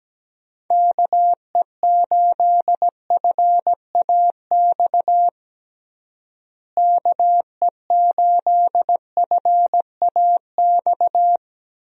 Morse code, heard as a keyed tone, about 17 wpm, 700 Hz. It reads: KE8FAX KE8FAX
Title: USA Call Signs - 6 Characters - Repeated 3x - 17wpm